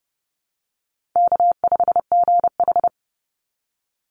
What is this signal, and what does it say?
Morse code, heard as a keyed tone, about 32 wpm, 700 Hz. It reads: K5GH
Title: Easy Call Signs - 30wpm